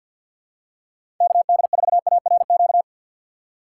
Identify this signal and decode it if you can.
Morse code, heard as a keyed tone, about 52 wpm, 700 Hz. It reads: KD4ARQ
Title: USA Call Signs - 6 Characters - 4x Character Spacing - 50wpm